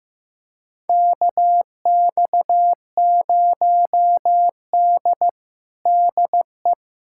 Morse code, heard as a keyed tone, about 16 wpm, 700 Hz. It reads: KX0D DE
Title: USA Callsign DE Callsign - 4 Characters - 15wpm